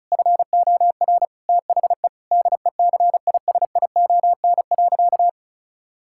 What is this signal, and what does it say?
Morse code, heard as a keyed tone, about 36 wpm, 700 Hz. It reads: FOR THE DECISION.
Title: Sentences from Top 1600 Words 35wpm